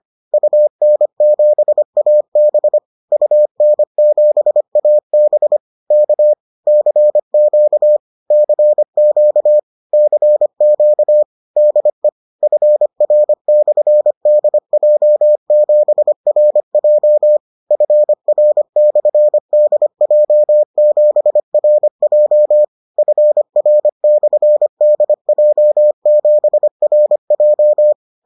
Morse code, heard as a keyed tone, about 25 wpm, 600 Hz. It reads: UN7AB UN7AB K CQ CQ CQ DE FR/DJ7RJ FR/DJ7RJ FR/DJ7RJ